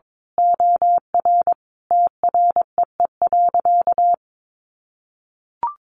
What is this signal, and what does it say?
Morse code, heard as a keyed tone, about 22 wpm, 700 Hz. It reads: OR TREE.
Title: A Sound Of Thunder 22wpm